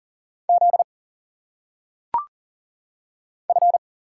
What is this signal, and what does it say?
Morse code, heard as a keyed tone, about 40 wpm, 700 Hz. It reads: Z F
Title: Single Letters - Phonetic Pronunciation 40wpm